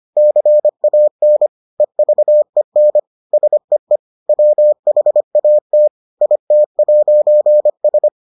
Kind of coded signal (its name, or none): Morse code